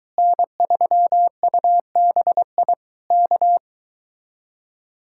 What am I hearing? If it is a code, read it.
Morse code, heard as a keyed tone, about 22 wpm, 700 Hz. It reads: N3UBI K